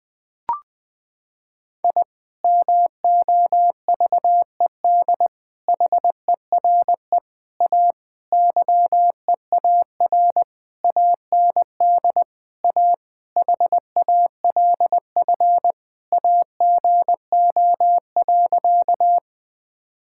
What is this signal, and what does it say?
Morse code, heard as a keyed tone, about 20 wpm, 700 Hz. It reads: I MOVED HERE A YEAR AND A HALF AGO.
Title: Sentences with 6-Word Phrases - 20wpm